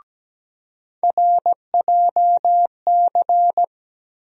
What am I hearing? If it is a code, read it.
Morse code, heard as a keyed tone, about 17 wpm, 700 Hz. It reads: RJC
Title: USA Call Sign Postfixes - 3 Letters 17wpm